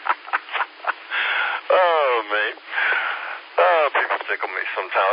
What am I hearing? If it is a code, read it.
Emergency Alert System (SAME) tones, heard as harsh buzzing bursts, with speech. It is heard from 1.0 to 4.7 s.